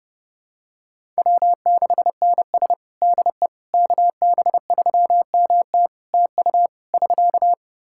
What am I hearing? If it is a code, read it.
Morse code, heard as a keyed tone, about 30 wpm, 700 Hz. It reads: W6NS DE KB3MT TU <SK>
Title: Practice QSO - Exchange 4 - Ending - 30wpm